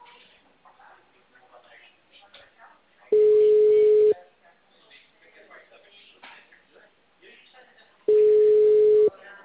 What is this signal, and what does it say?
Ringback tone (Europe, 425 Hz), with speech.